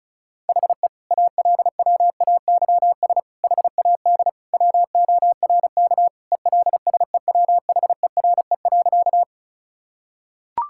Morse code, heard as a keyed tone, about 36 wpm, 690 Hz. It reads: HE ALWAYS HAD WORK ELSEWHERE.
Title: Sentences from Top 1600 Words 35wpm